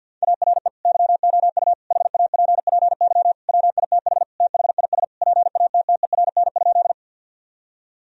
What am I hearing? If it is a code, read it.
Morse code, heard as a keyed tone, about 50 wpm, 700 Hz. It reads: ARE YOU HAPPY WITH THIS PATTERN?
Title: Sentences from Top 1000 Words - Review Part 2 - 50wpm